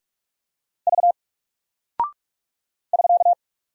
Morse code, heard as a keyed tone, about 44 wpm, 700 Hz. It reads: V <SK>
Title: Single Character (Letter:Number:Punctuation:Prosign) Alt 45wpm